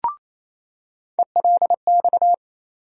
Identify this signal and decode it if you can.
Morse code, heard as a keyed tone, about 28 wpm, 700 Hz. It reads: ELX
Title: USA Call Sign Postfixes - 3 Letters 28wpm